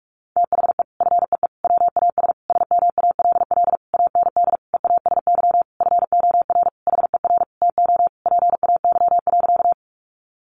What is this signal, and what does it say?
Morse code, heard as a keyed tone, about 44 wpm, 700 Hz. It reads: THE FEE WAS SMALL AND EASY FOR HER TO PAY.
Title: Sentences from Top 1500 Words 45wpm